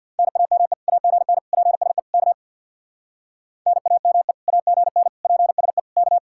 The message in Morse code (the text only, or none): NAME AGN PSE K NAME AGN PSE K